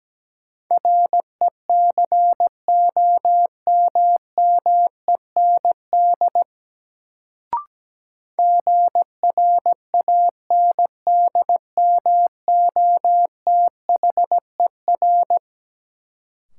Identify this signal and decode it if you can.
Morse code, heard as a keyed tone, about 17 wpm, 700 Hz. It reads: RECOMMEND GRANDMOTHER
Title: Sentences from Top 1200 Words - New Word Drill - 17wpm